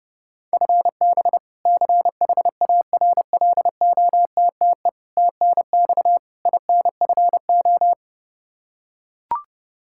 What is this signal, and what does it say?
Morse code, heard as a keyed tone, about 29 wpm, 700 Hz. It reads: FB CHARLOTTE TNX INFO
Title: Practice QSO - Exchange 3 - More Info - 30wpm